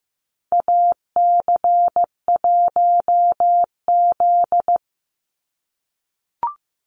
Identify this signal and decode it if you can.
Morse code, heard as a keyed tone, about 15 wpm, 700 Hz. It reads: AC1Z
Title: NA SOTA Activator Callsigns 15wpm